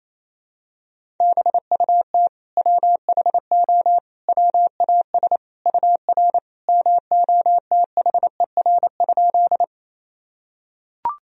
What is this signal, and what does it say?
Morse code, heard as a keyed tone, about 28 wpm, 700 Hz. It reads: BUT WHO WAS UR MOTHER?
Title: Sentences from Top 200 Words and Abbreviations 28wpm